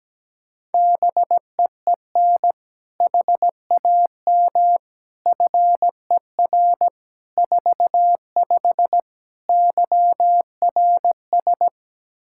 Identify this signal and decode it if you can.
Morse code, heard as a keyed tone, about 17 wpm, 700 Hz. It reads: BEEN HAM FER 45 YRS
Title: Practice QSO - Exchange 3 - More Info - 17wpm